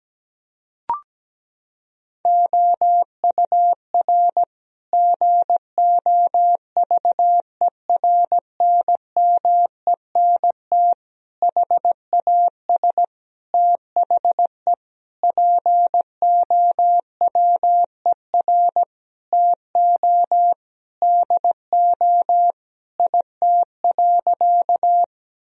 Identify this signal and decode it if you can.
Morse code, heard as a keyed tone, about 17 wpm, 700 Hz. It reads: OUR GOVERNMENT HAS THE POWER TO DO IT.
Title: Sentences from Top 1000 Words - Review Part 3 - 17wpm